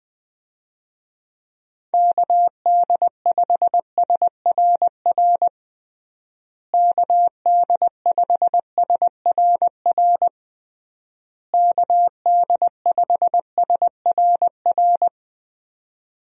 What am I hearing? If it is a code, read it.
Morse code, heard as a keyed tone, about 19 wpm, 700 Hz. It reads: KD5SRR KD5SRR KD5SRR
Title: USA Call Signs - 6 Characters - Repeated 3x - 20wpm